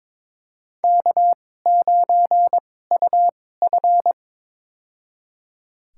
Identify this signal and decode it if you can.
Morse code, heard as a keyed tone, about 21 wpm, spaced as for 16 wpm, 700 Hz. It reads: K9UF
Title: USA Call Signs - 2x Character Spacing - 22wpm